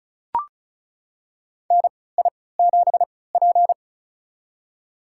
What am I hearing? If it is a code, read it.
Morse code, heard as a keyed tone, about 34 wpm, spaced as for 20 wpm, 700 Hz. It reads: NI7P
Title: USA Call Signs - 4 Characters - 3x Character Spacing - 35wpm